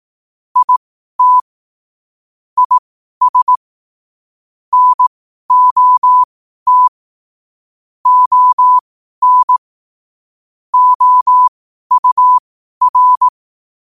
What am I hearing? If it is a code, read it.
Morse code, heard as a keyed tone, about 18 wpm, spaced as for 12 wpm, 1000 Hz. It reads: IT IS NOT ON OUR
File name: qotdfiles / 12wpm